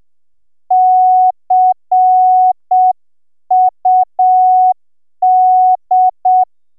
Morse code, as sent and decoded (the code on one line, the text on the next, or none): -.-. ..- -..
CUD